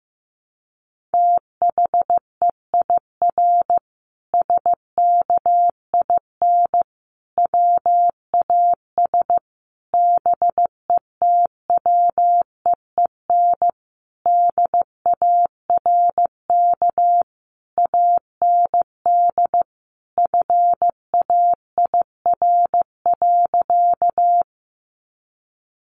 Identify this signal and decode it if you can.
Morse code, heard as a keyed tone, about 14 wpm, 700 Hz. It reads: THEIR SKIN WAS BETWEEN DARK AND FAIR.
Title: Sentences from Top 1300 Words 15wpm